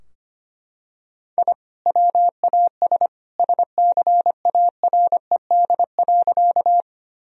Morse code, heard as a keyed tone, about 25 wpm, 700 Hz. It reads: I WAS SCARED.
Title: Sentences from Top 1000 Words - Review Part 4 - 25wpm